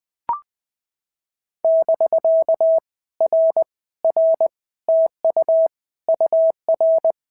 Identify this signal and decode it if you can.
Morse code, heard as a keyed tone, about 19 wpm, 650 Hz. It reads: <BK> R R TU UR
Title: POTA QSO Practice - 20wpm